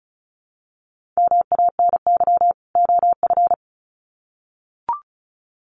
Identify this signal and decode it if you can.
Morse code, heard as a keyed tone, about 34 wpm, 700 Hz. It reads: MANY OF
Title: Sets of 2 Words 35wpm